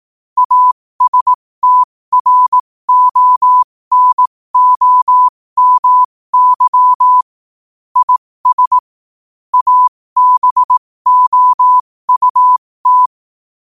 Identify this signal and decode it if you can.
Morse code, heard as a keyed tone, about 18 wpm, spaced as for 15 wpm, 1000 Hz. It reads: ASTRONOMY IS ABOUT
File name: qotdfiles / 15wpm